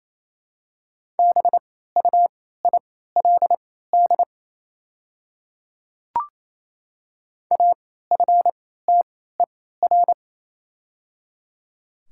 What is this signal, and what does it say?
Morse code, heard as a keyed tone, about 28 wpm, spaced as for 16 wpm, 700 Hz. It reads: BUILD AFTER
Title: 3x Character Spacing - Top 200 Words - 28wpm